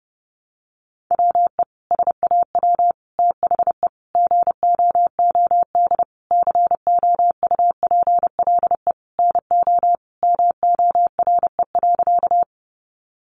Morse code, heard as a keyed tone, about 30 wpm, 700 Hz. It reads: WE SAW THE GOOD COUPLE NO MORE.
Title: Sentences from Top 1100 Words 30wpm